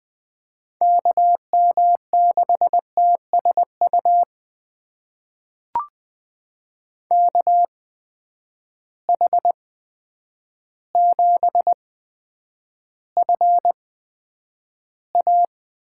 Morse code, heard as a keyed tone, about 20 wpm, spaced as for 9 wpm, 700 Hz. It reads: KM6TSU KH7FA
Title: USA Call Signs - 8x Character Spacing - 20wpm